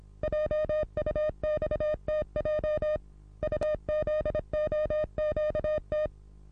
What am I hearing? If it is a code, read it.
Morse code, heard as a keyed tone, about 26 wpm, 610 Hz. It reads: JUXTJ UZOQT
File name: repos / Morsekurs / DJ4UF Morsen27